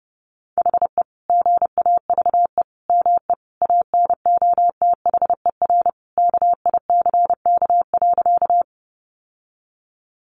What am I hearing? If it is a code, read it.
Morse code, heard as a keyed tone, about 30 wpm, 700 Hz. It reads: HE GAVE ME ANOTHER KICK.
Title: Sentences from Top 1300 Words 30wpm